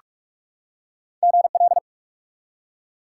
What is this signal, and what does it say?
Morse code, heard as a keyed tone, about 47 wpm, 700 Hz. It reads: GL
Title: Top 90 QSO Elements 45wpm